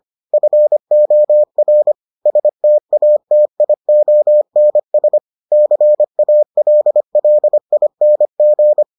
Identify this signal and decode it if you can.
Morse code, heard as a keyed tone, about 25 wpm, 600 Hz. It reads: FOR STATIONS CALLING